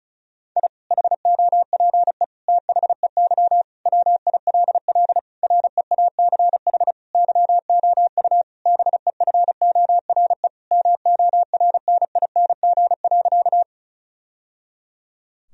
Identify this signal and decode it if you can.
Morse code, heard as a keyed tone, about 36 wpm, 700 Hz. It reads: I HOPE THEY WILL REACH YOU BEFORE MORNING.
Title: Sentences from Top 1000 Words - Review Part 2 - 35wpm